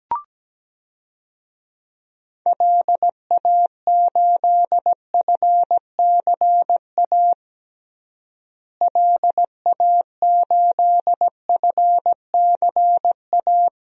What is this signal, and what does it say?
Morse code, heard as a keyed tone, about 17 wpm, 700 Hz. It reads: LA8FCA LA8FCA
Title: DXCC-100 Call Signs - 6 Characters - Repeated 2x - 17wpm